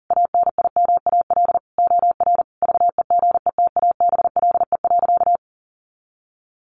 Morse code, heard as a keyed tone, about 40 wpm, 700 Hz. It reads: ANIMAL OR VEGETABLE.
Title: Sentences from Top 1300 Words 40wpm